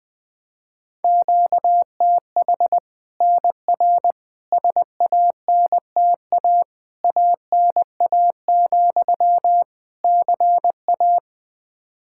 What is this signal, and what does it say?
Morse code, heard as a keyed tone, about 20 wpm, 700 Hz. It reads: QTH NR SANTA ANA, CA
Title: Practice QSO - Exchange 1 - Beginning - 20wpm